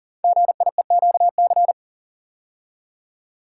Morse code, heard as a keyed tone, about 40 wpm, 700 Hz. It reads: GIEQC
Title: Code Groups - 5 Letter:Numbers - 40wpm